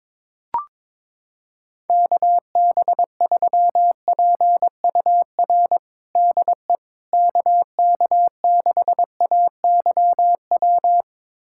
Morse code, heard as a keyed tone, about 22 wpm, 700 Hz. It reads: KB3PUR DE KK6AYW
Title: USA Callsign DE Callsign - 6 Characters - 22wpm